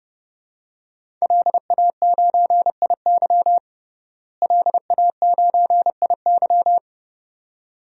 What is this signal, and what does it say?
Morse code, heard as a keyed tone, about 32 wpm, 700 Hz. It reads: LA9IY LA9IY
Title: DXCC-100 Call Signs - 5 Characters - Repeated 2x - 30wpm